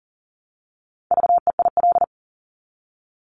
Morse code, heard as a keyed tone, about 42 wpm, 700 Hz. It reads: VEIL
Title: Multi-letter Phonemes Repeated 3x 40wpm